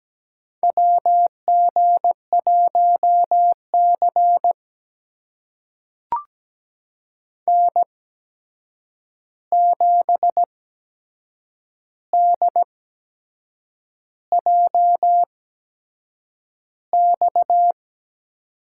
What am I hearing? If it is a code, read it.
Morse code, heard as a keyed tone, about 17 wpm, spaced as for 7 wpm, 700 Hz. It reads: WG1C N7DJX
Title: USA Call Signs - 8x Character Spacing - 17wpm